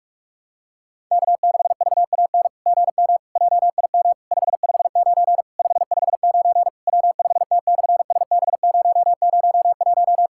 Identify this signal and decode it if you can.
Morse code, heard as a keyed tone, about 48 wpm, 700 Hz. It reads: K6VAN GM JIM 559 559 W5T/SB001
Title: SOTA QSO Practice 45wpm